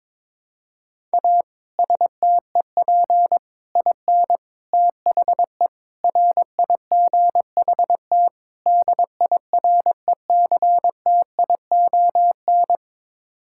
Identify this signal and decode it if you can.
Morse code, heard as a keyed tone, about 22 wpm, 700 Hz. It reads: A STEP IN THE RIGHT DIRECTION
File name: practice / 22wpm / Sets of 6 Words - 22wpm